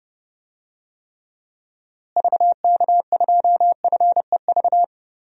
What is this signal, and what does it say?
Morse code, heard as a keyed tone, about 31 wpm, 700 Hz. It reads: VK2FEV
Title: DXCC-100 Call Signs - 6 Characters - Repeated 2x - 30wpm